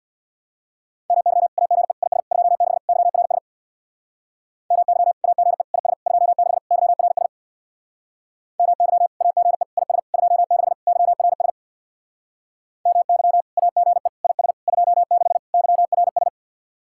Morse code, heard as a keyed tone, about 64 wpm, 700 Hz. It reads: MY AGE IS 26 YRS MY AGE IS 26 YRS MY AGE IS 26 YRS MY AGE IS 26 YRS